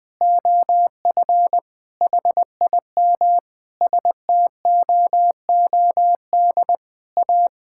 Morse code, heard as a keyed tone, about 20 wpm, 700 Hz. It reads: OF HIM STOOD A
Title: Sentences with 4-Word Phrases - 20wpm